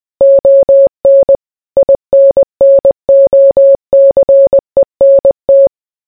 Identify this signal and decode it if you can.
Morse code, heard as a keyed tone, about 20 wpm, 550 Hz. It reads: ON INNOCENT